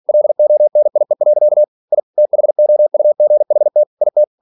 Morse code, heard as a keyed tone, about 46 wpm, 600 Hz. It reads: LONIE. I THOUGHT IT
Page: Never 47wpm